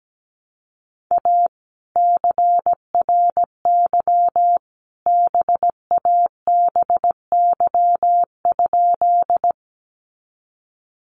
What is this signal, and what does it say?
Morse code, heard as a keyed tone, about 17 wpm, 700 Hz. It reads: A CRY BABY?
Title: Sentences from Top 1300 Words 17wpm